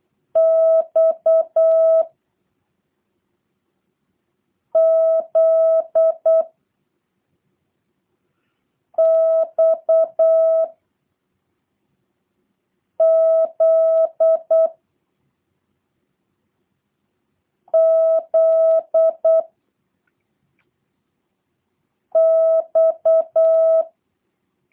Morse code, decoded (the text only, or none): X Z X Z Z X